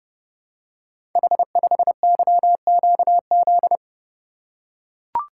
Morse code, heard as a keyed tone, about 29 wpm, 700 Hz. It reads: H5YQZ
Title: Code Groups - 5 Letter:Numbers - 30wpm